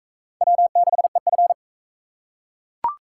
Morse code, heard as a keyed tone, about 37 wpm, 700 Hz. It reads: W6EF